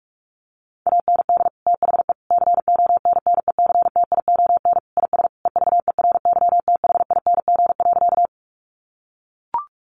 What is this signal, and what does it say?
Morse code, heard as a keyed tone, about 44 wpm, 700 Hz. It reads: AND THE CONNECTION IS EVERYTHING.
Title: Sentences from Top 1300 Words 45wpm